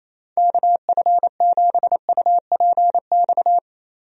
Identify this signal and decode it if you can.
Morse code, heard as a keyed tone, about 27 wpm, 700 Hz. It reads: KF7UPX
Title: USA Call Signs - 6 Characters - 28wpm